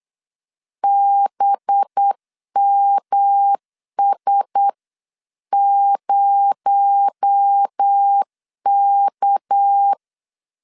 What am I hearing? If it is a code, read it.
Morse code, heard as a keyed tone, about 8 wpm, 800 Hz. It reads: BMS0K
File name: CW / DET / 1 6 1500C S B 8wpm 5142